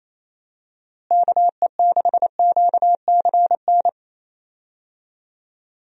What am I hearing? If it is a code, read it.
Morse code, heard as a keyed tone, about 28 wpm, 700 Hz. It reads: KE6QCN
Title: USA Call Signs - 6 Characters - Repeated 2x - 28wpm